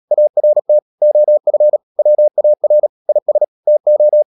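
Morse code, heard as a keyed tone, about 37 wpm, 600 Hz. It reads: ART OF WAR IS TO
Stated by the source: Morse code practice at 35–39 wpm sending English text of Ken Follett's novel Never, from CW med Gnister.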